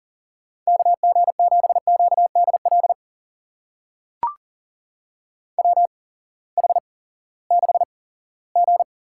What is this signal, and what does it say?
Morse code, heard as a keyed tone, about 40 wpm, spaced as for 20 wpm, 700 Hz. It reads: KG7QDL WH6G
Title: USA Call Signs - 8x Character Spacing - 40wpm